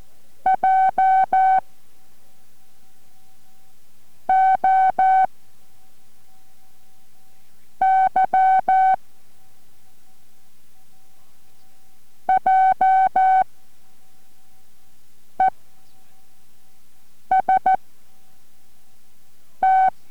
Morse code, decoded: J O Y J E S T